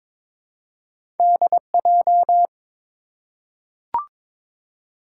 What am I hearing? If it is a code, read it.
Morse code, heard as a keyed tone, about 22 wpm, 700 Hz. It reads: DJ
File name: practice / 22wpm / Two Letter-Number Combinations - 22wpm